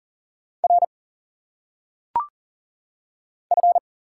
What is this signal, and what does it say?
Morse code, heard as a keyed tone, about 40 wpm, 700 Hz. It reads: R F
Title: Single Letters - Phonetic Pronunciation 40wpm